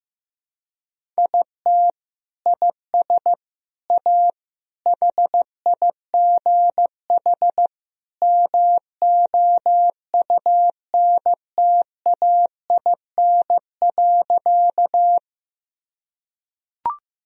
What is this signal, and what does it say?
Morse code, heard as a keyed tone, about 15 wpm, 700 Hz. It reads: IT IS A HIGH MOUNTAIN.
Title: Sentences from Top 1000 Words - Review Part 1 - 15wpm